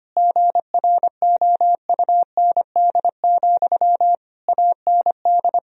Morse code, heard as a keyed tone, about 25 wpm, 700 Hz. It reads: GROUND, AND
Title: Sentences from Top 1000 Words - Review Part 2 - 25wpm